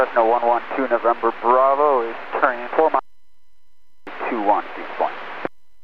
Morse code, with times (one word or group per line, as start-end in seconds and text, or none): none